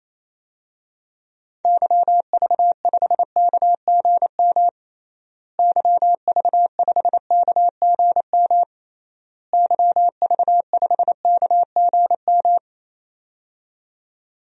Morse code, heard as a keyed tone, about 27 wpm, 700 Hz. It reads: YV5KGM YV5KGM YV5KGM
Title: DXCC-100 Call Signs - 6 Characters - Repeated 3x - 28wpm